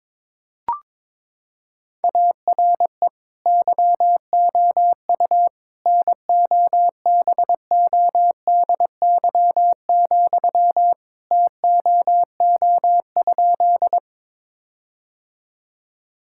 Morse code, heard as a keyed tone, about 22 wpm, 700 Hz. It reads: ARE YOU NOBODY, TOO?
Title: Sentences from Top 1600 Words 22wpm